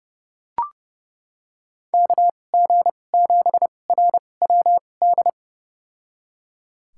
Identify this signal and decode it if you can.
Morse code, heard as a keyed tone, about 30 wpm, spaced as for 22 wpm, 700 Hz. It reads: KG7RWD
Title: USA Call Signs - 2x Character Spacing - 30wpm